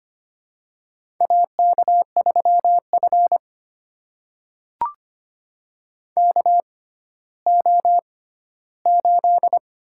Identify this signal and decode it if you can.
Morse code, heard as a keyed tone, about 25 wpm, spaced as for 14 wpm, 700 Hz. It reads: AK3F KO8
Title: USA Call Signs - 6x Character Spacing - 25wpm